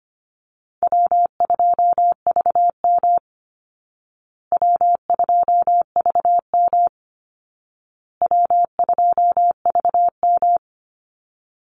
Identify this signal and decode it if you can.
Morse code, heard as a keyed tone, about 25 wpm, 700 Hz. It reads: W2VM W2VM W2VM
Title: Easy Call Signs Repeated 3x - 25wpm